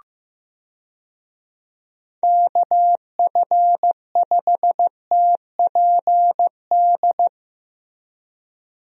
Morse code, heard as a keyed tone, about 15 wpm, 700 Hz. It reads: KF5TPD
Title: USA Call Signs - 6 Characters - Repeated 2x - 15wpm